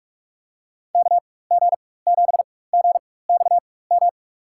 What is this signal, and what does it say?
Morse code, heard as a keyed tone, about 47 wpm, spaced as for 21 wpm, 700 Hz. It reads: KG7GXM
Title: USA Call Signs - 4x Character Spacing - 45wpm